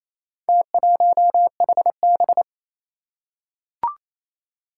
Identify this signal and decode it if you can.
Morse code, heard as a keyed tone, about 27 wpm, 700 Hz. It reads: T1HB
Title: Four Challenging Letter-Number Combinations 28wpm